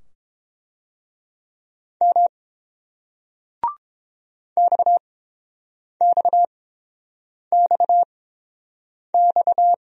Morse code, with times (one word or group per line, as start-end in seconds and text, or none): 2.0-2.3 M
4.6-9.7 XXXX